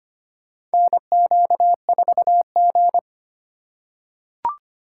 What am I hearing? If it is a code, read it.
Morse code, heard as a keyed tone, about 25 wpm, 700 Hz. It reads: NQ4G
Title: USA Call Signs - 4 Characters - 3x Character Spacing - 25wpm